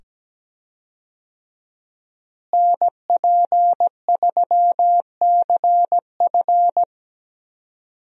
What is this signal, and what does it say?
Morse code, heard as a keyed tone, about 17 wpm, 700 Hz. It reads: NP3CF
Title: USA Call Signs - 5 Characters - Repeated 2x - 17wpm